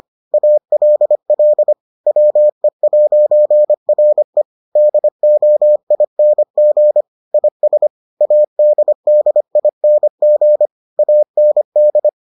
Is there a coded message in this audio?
Yes: Morse code from 0.3 to 12.1 s, heard as a keyed tone, about 25 wpm, 600 Hz. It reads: ALL WE'RE DOING IS ADDING AND